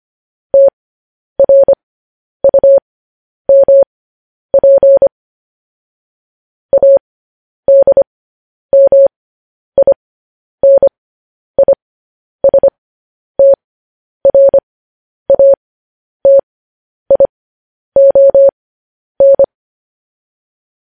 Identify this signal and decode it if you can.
Morse code, heard as a keyed tone, about 25 wpm, spaced as for 10 wpm, 550 Hz. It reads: TRUMP ADMINISTRATION
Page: News Headlines in Morse Code at 10 WPM